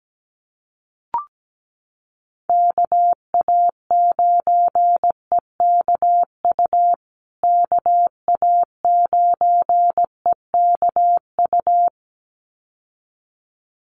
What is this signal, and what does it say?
Morse code, heard as a keyed tone, about 17 wpm, 700 Hz. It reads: KA9EKU KA9EKU
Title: USA Call Signs Repeated 2x 17wpm